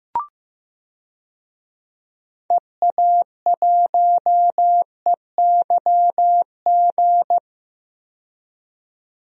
Morse code, heard as a keyed tone, about 15 wpm, 700 Hz. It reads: EA1EYG
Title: DXCC-100 Call Signs - 6 Characters - 15wpm